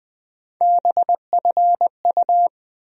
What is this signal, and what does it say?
Morse code, heard as a keyed tone, about 20 wpm, 700 Hz. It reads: BFU
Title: USA Call Sign Postfixes - 3 Letters 20wpm